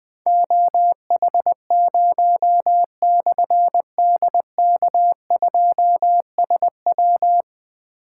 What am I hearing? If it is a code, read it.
Morse code, heard as a keyed tone, about 20 wpm, 700 Hz. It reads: OH0/DK2SW
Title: Difficult Call Signs 20wpm